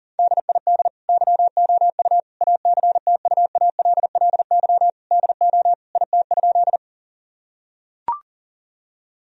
Morse code, heard as a keyed tone, about 40 wpm, 700 Hz. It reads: DID YOU ACTUALLY DO IT?
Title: Sentences from Top 1000 Words - Review Part 4 - 40wpm